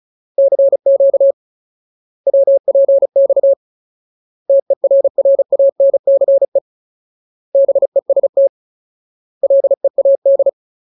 Morse code, heard as a keyed tone, about 34 wpm, 550 Hz. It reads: CQ WPX TERRANCE BEST LEAD
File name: Ultimate Word List / 35WPM Ultimate